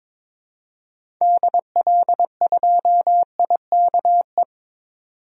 Morse code, heard as a keyed tone, about 22 wpm, 700 Hz. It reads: DL2IKE
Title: DXCC-100 Call Signs - 6 Characters - 22wpm